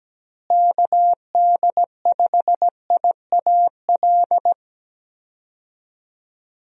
Morse code, heard as a keyed tone, about 17 wpm, 700 Hz. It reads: KD5IAL
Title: USA Call Signs - 6 Characters - Repeated 2x - 17wpm